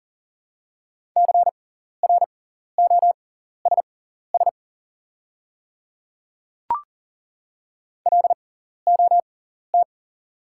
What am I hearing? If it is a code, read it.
Morse code, heard as a keyed tone, about 40 wpm, spaced as for 14 wpm, 690 Hz. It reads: CROSS LOT
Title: 6x Character Spacing - Top 500 Words - 40wpm